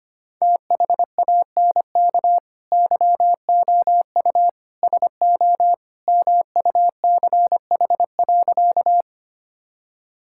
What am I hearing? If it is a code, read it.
Morse code, heard as a keyed tone, about 25 wpm, 700 Hz. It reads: THANK YOU SO MUCH.
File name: practice / 25wpm / Sentences with 2-Word Phrases - 25wpm